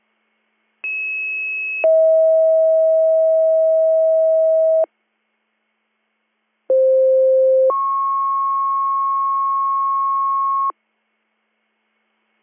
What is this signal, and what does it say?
Two-tone page, heard as a one-second tone, then a three-second tone, about 2600 then 640 Hz, about 520 then 1090 Hz. It is heard from 0.8 to 10.7 s.